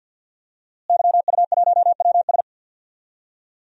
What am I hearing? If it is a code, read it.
Morse code, heard as a keyed tone, about 47 wpm, 700 Hz. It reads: YU1WS
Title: DXCC-100 Call Signs - 5 Characters - Repeated 2x - 50wpm